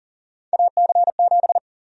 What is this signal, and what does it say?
Morse code, heard as a keyed tone, about 40 wpm, 700 Hz. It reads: AC7
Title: USA Call Sign Prefixes - 40wpm